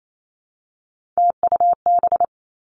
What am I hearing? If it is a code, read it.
Morse code, heard as a keyed tone, about 28 wpm, 700 Hz. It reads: TUB
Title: Three Challenging Letter-Number Combinations 28wpm